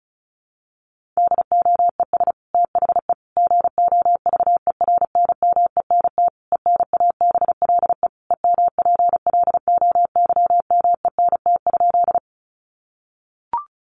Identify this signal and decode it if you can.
Morse code, heard as a keyed tone, about 34 wpm, 700 Hz. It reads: DOES THE GOVERNMENT ENABLE EMPLOYMENT?
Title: Sentences from Top 1500 Words 35wpm